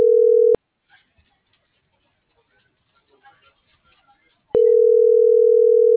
Ringback tone (North America), 440 + 480 Hz.
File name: CENTRAL SKYBLUE / out lorena.gonzalez